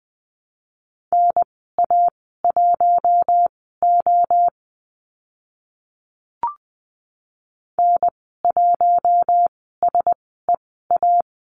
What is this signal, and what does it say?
Morse code, heard as a keyed tone, about 20 wpm, spaced as for 15 wpm, 700 Hz. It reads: NA1O N1SEA
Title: 042 Lesson 8g - Number 1 - Call Signs 20wpm